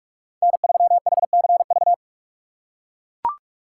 Morse code, heard as a keyed tone, about 44 wpm, 700 Hz. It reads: N3HCV
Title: USA Call Signs - 5 Characters - 6x Character Spacing - 45wpm